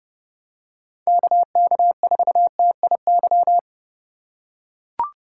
Morse code, heard as a keyed tone, about 32 wpm, 700 Hz. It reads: KK4TIY
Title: USA Call Signs - 6 Characters - 4x Character Spacing - 30wpm